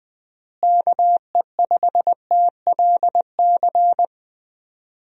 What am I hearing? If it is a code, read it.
Morse code, heard as a keyed tone, about 20 wpm, 700 Hz. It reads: KE5TLC
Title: USA Call Signs - 6 Characters - Repeated 3x - 20wpm